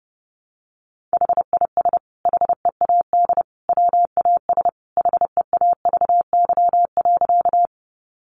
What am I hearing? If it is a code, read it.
Morse code, heard as a keyed tone, about 30 wpm, 700 Hz. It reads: HIS HEAD WAS HEAVY.
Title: Sentences from Top 1100 Words 30wpm